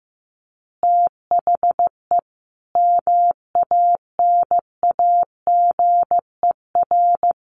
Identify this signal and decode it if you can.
Morse code, heard as a keyed tone, about 15 wpm, 700 Hz. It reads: THE MANAGER
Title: Additional Easy Phrases 15wpm